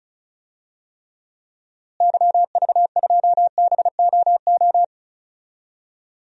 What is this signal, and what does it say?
Morse code, heard as a keyed tone, about 36 wpm, 700 Hz. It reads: YV2BOO
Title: DXCC-100 Call Signs - 6 Characters - Repeated 2x - 35wpm